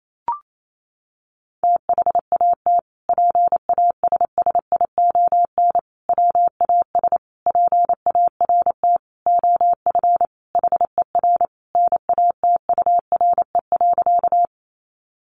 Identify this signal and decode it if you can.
Morse code, heard as a keyed tone, about 28 wpm, 700 Hz. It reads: THAT PASSION WAS PART OF HER NATURE.
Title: Sentences from Top 1300 Words 28wpm